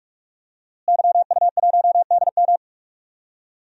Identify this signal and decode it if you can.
Morse code, heard as a keyed tone, about 46 wpm, 700 Hz. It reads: YU1DM
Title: DXCC-100 Call Signs - 5 Characters - Repeated 3x - 45wpm